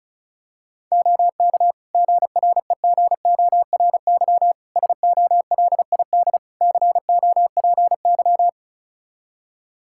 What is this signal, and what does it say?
Morse code, heard as a keyed tone, about 36 wpm, 700 Hz. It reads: OK GREGORY SOLID COPY
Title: Practice QSO - Exchange 3 - More Info - 35wpm